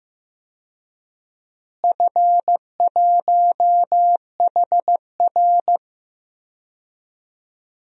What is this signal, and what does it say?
Morse code, heard as a keyed tone, about 15 wpm, 700 Hz. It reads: F1HR
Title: DXCC-100 Call Signs - 4 Characters - Repeated 3x - 15wpm